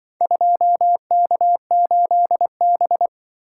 Morse code, heard as a keyed tone, about 24 wpm, 700 Hz. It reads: 2K8B
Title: Speed-Racing - Four Letter-Number Combinations - 20wpm